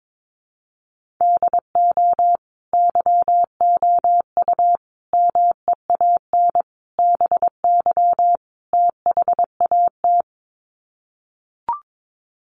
Morse code, heard as a keyed tone, about 22 wpm, 700 Hz. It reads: DO YOU MEAN BY THAT
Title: Sets of 5 Words Encore2 22wpm